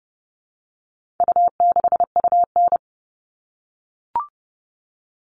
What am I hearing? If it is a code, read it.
Morse code, heard as a keyed tone, about 30 wpm, 700 Hz. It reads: U6UN
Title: All Cut Numbers - 4-Digit 30wpm